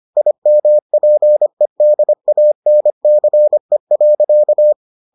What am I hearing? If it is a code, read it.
Morse code, heard as a keyed tone, about 25 wpm, 600 Hz. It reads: IMPEDANCE.